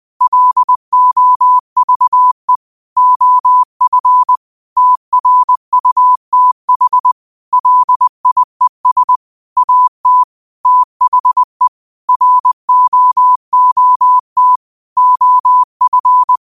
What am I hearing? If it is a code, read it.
Morse code, heard as a keyed tone, about 20 wpm, 1000 Hz. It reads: LOVE OF TRUTH LIES AT THE ROOT OF